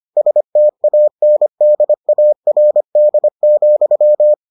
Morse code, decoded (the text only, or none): STANDARD,